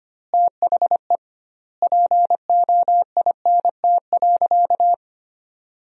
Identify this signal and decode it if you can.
Morse code, heard as a keyed tone, about 25 wpm, 700 Hz. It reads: THE POINT.
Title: 2x Word-Spacing - Sentences from Top 200 Words - 25wpm